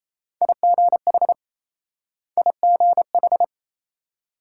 Morse code, heard as a keyed tone, about 38 wpm, 700 Hz. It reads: IGH IGH